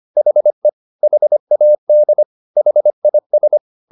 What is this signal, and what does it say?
Morse code, heard as a keyed tone, about 25 wpm, 600 Hz. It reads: HE HAD HIS
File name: the old man and the sea part 1 25wpm